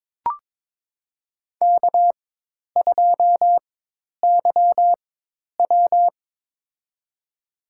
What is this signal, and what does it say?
Morse code, heard as a keyed tone, about 23 wpm, spaced as for 10 wpm, 700 Hz. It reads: K2YW
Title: USA Call Signs - 4x Character Spacing - 22wpm